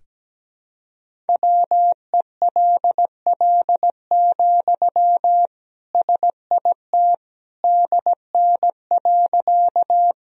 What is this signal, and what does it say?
Morse code, heard as a keyed tone, about 17 wpm, 700 Hz. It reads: WELL, SIT DN.
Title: Sentences from Top 500 Words and Abbreviations 17wpm